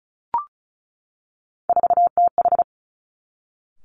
Morse code, heard as a keyed tone, about 36 wpm, 700 Hz. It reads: 4TH
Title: Three Challenging Letter-Number Combinations 35wpm